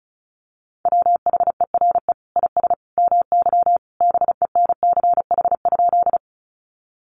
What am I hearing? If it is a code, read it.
Morse code, heard as a keyed tone, about 34 wpm, 700 Hz. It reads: WHERE IS MY BENCH?
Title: Sentences from Top 1400 Words 35wpm